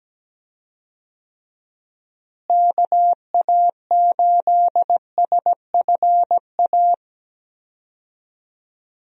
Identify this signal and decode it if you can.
Morse code, heard as a keyed tone, about 17 wpm, 700 Hz. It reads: KA8SFA
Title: USA Call Signs - 6 Characters - Repeated 2x - 17wpm